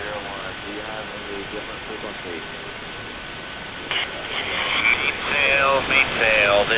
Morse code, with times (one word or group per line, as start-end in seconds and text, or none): none